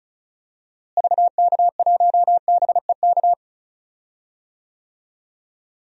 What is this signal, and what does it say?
Morse code, heard as a keyed tone, about 36 wpm, 700 Hz. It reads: VK1BEK
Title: DXCC-100 Call Signs - 6 Characters - Repeated 3x - 35wpm